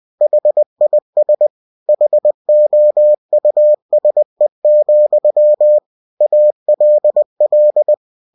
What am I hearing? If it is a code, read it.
Morse code, heard as a keyed tone, about 20 wpm, 600 Hz. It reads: HIS HOUSE, ALL
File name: cw-books / winnie-the-pooh / winnie-20wpm-page-0092